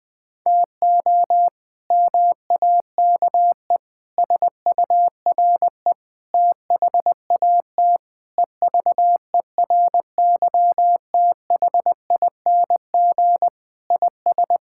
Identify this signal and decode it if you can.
Morse code, heard as a keyed tone, about 21 wpm, 700 Hz. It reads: TO MAKE SURE THAT EVERYTHING IS
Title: Sentences with 4-Word Phrases - Spoken Beginning - 20wpm